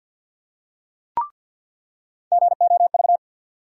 Morse code, heard as a keyed tone, about 53 wpm, 700 Hz. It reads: GOV
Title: USA Call Sign Postfixes - 3 Letters 50wpm